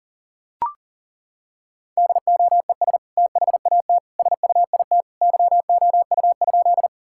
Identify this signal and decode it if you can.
Morse code, heard as a keyed tone, about 40 wpm, 700 Hz. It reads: DOES THAT SUIT YOU?
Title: Sentences from Top 1600 Words 40wpm